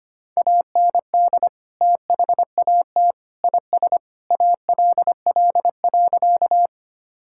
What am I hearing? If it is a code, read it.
Morse code, heard as a keyed tone, about 25 wpm, 700 Hz. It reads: AND THAT IS ALL.
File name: practice / 25wpm / Sentences with 3-Word Phrases - 25wpm